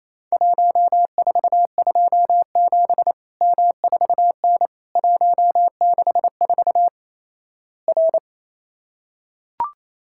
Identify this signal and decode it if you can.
Morse code, heard as a keyed tone, about 28 wpm, 700 Hz. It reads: 1427 M4N 164 R
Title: Worked All Europe DX Contest - QTC Exchange - 28wpm